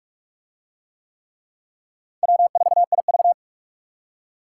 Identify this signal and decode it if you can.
Morse code, heard as a keyed tone, about 38 wpm, 700 Hz. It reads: W4IV